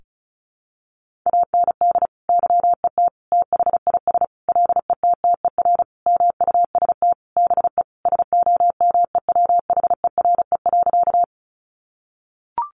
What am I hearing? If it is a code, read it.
Morse code, heard as a keyed tone, about 34 wpm, 700 Hz. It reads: AND YET THIS LETTER MUST BE SOMEWHERE.
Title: Sentences from Top 1400 Words 35wpm